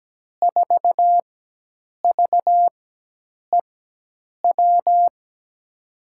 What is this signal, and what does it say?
Morse code, heard as a keyed tone, about 17 wpm, spaced as for 8 wpm, 700 Hz. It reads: 4VEW
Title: USA Call Signs - 6 Characters - 4x Character Spacing - 17wpm